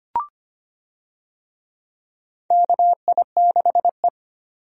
Morse code, heard as a keyed tone, about 25 wpm, 700 Hz. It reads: KI6E
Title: USA Call Signs - 4 Characters - 25wpm